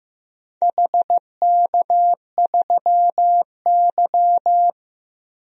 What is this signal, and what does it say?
Morse code, heard as a keyed tone, about 15 wpm, 700 Hz. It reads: HK3Y
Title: DXCC-100 Call Signs - 4 Characters - 15wpm